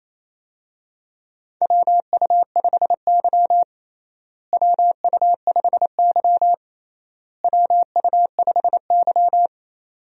Morse code, heard as a keyed tone, about 27 wpm, 700 Hz. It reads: WU5Y WU5Y WU5Y
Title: USA Call Signs - 4 Characters - Repeated 3x - 28wpm